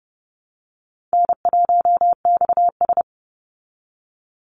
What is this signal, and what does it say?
Morse code, heard as a keyed tone, about 30 wpm, 700 Hz. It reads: N1XS
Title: Easy Call Signs Repeated 2x - 30wpm